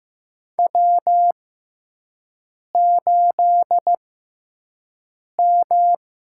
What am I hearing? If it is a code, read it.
Morse code, heard as a keyed tone, about 15 wpm, spaced as for 5 wpm, 700 Hz. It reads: W8M
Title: USA Call Signs - 5 Characters - 6x Character Spacing - 15wpm